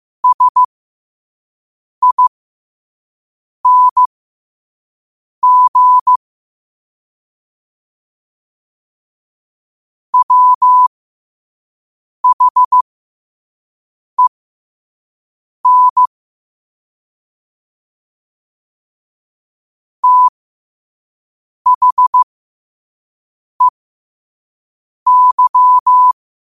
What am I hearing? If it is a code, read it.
Morse code, heard as a keyed tone, about 15 wpm, spaced as for 5 wpm, 1000 Hz. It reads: SING WHEN THEY
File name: qotdfiles / 5wpm